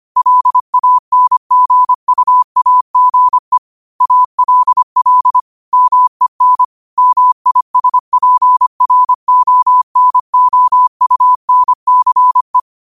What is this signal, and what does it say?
Morse code, heard as a keyed tone, about 25 wpm, 1000 Hz. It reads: LANGUAGE ALL MEN MISPRONOUNCE
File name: qotdfiles / 25wpm